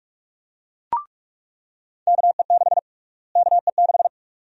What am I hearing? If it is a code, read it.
Morse code, heard as a keyed tone, about 43 wpm, 700 Hz. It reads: KE6 KE6
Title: USA Call Sign Prefixes - Repeated 2x - 45wpm